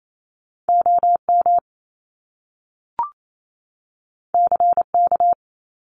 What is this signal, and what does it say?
Morse code, heard as a keyed tone, about 28 wpm, 700 Hz. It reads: OM CK
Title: Top Two Letter Combinations in Words 28wpm